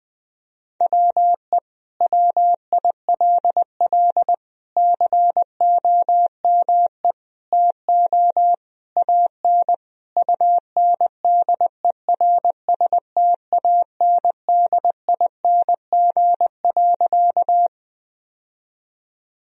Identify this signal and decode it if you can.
Morse code, heard as a keyed tone, about 20 wpm, 700 Hz. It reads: WE WILL COME TO AN UNDERSTANDING.
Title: Sentences from Top 1000 Words - Review Part 3 - 20wpm